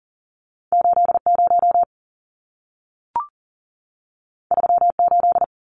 Morse code, heard as a keyed tone, about 40 wpm, 700 Hz. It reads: 80 38
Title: Numbers - 2-Digit 40wpm